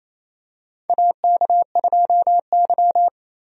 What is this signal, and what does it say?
Morse code, heard as a keyed tone, about 28 wpm, 700 Hz. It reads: AK2Y
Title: USA Call Signs - 4 Characters - 28wpm